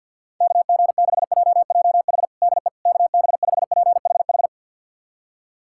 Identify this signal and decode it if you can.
Morse code, heard as a keyed tone, about 52 wpm, 700 Hz. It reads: KG6JJH DE KB5PHH
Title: USA Callsign DE Callsign 50wpm